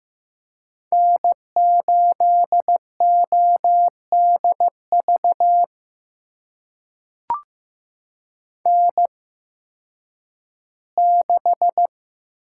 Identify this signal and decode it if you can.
Morse code, heard as a keyed tone, about 15 wpm, spaced as for 10 wpm, 700 Hz. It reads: N8ODV N6
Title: USA Call Signs - 5 Characters - 8x Character Spacing - 15wpm